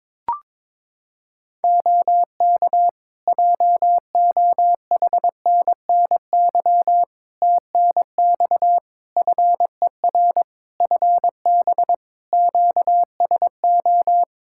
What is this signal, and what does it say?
Morse code, heard as a keyed tone, about 22 wpm, 700 Hz. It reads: OK JOHNNY TNX FER FB QSO
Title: Practice QSO - Exchange 4 - Ending - 22wpm